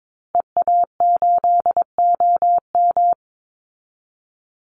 Morse code, heard as a keyed tone, about 22 wpm, 700 Hz. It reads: EA8OM
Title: Call Signs Repeated 3x - 22wpm